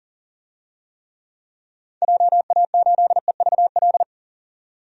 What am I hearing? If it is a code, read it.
Morse code, heard as a keyed tone, about 37 wpm, 700 Hz. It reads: JA8EVL
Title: DXCC-100 Call Signs - 6 Characters - 40wpm